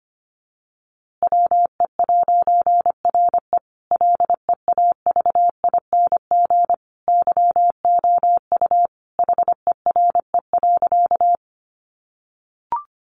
Morse code, heard as a keyed tone, about 25 wpm, 700 Hz. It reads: WE'RE LEAVING YOU HERE.
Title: A Sound Of Thunder 25wpm